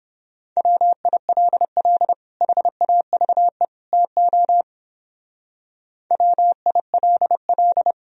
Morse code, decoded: WILL HAVE TO WILL

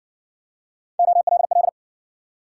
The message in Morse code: OLL

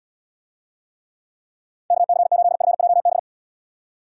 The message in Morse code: KC8FJB

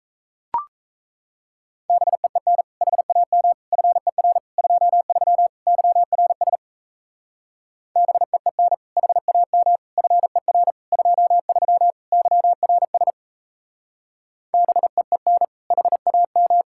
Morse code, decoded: BEEN HAM FER 23 YRS BEEN HAM FER 23 YRS BEEN HAM

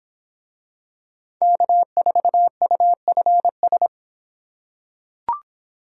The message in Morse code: K4UFS

